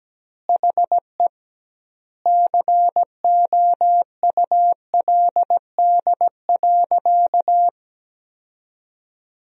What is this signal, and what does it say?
Morse code, heard as a keyed tone, about 17 wpm, 700 Hz. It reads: HE COULD.
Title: 2x Word-Spacing - Sentences from Top 500 Words - 17wpm